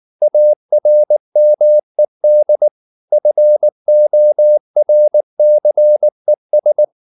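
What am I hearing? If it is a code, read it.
Morse code, heard as a keyed tone, about 19 wpm, 600 Hz. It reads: ARMED FORCES
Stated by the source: Morse code at 19 wpm from CW med Gnister, reading Ken Follett's novel Never in English.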